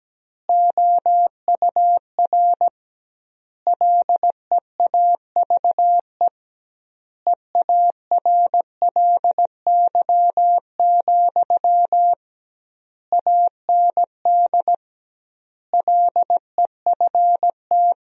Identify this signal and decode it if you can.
Morse code, heard as a keyed tone, about 17 wpm, 700 Hz. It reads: OUR LEAVE EARLY, AND LEFT
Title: 2x Word-Spacing - Sentences from Top 400 Words - 17wpm